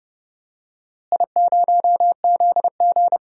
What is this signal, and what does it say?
Morse code, heard as a keyed tone, about 32 wpm, 700 Hz. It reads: I0ZG
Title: DXCC-100 Call Signs - 4 Characters - Repeated 3x - 30wpm